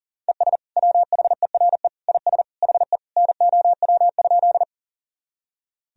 Morse code, heard as a keyed tone, about 40 wpm, 700 Hz. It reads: ES WHERE IS HE NOW?
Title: Sentences from Top 200 Words and Abbreviations 40wpm